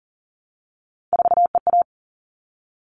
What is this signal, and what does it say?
Morse code, heard as a keyed tone, about 40 wpm, 700 Hz. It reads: 4EA
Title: All Cut Numbers - 3-Digit 40wpm